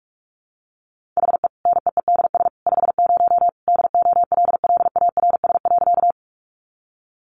Morse code, heard as a keyed tone, about 44 wpm, 700 Hz. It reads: HE NEEDS 50 DOLLARS.
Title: Additional Easy Phrases 45wpm